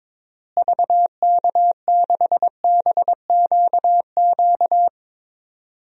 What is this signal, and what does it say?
Morse code, heard as a keyed tone, about 22 wpm, 700 Hz. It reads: VK6BQQ
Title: DXCC-100 Call Signs - 6 Characters - Repeated 2x - 22wpm